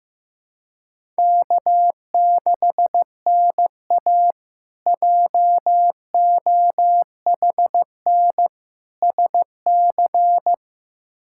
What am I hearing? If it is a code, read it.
Morse code, heard as a keyed tone, about 15 wpm, 700 Hz. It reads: K6NA JOHN SC
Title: K1USN SST QSO Practice - 15wpm